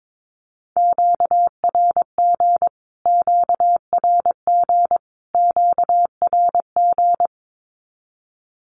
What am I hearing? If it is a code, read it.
Morse code, heard as a keyed tone, about 22 wpm, 700 Hz. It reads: QRG QRG QRG
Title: Q-Codes Repeated 3x 22wpm